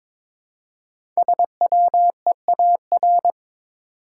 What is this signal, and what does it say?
Morse code, heard as a keyed tone, about 22 wpm, 690 Hz. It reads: SWEAR
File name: 5 Letter Words - First 3 Letters Given - 22wpm